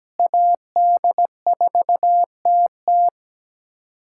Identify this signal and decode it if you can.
Morse code, heard as a keyed tone, about 17 wpm, 700 Hz. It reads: AD4TT
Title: K1USN SST QSO Practice - 17wpm